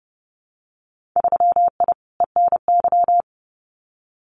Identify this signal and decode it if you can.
Morse code, heard as a keyed tone, about 30 wpm, 700 Hz. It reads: 3I ENY
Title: Winter Field Day Exchanges Encore 30wpm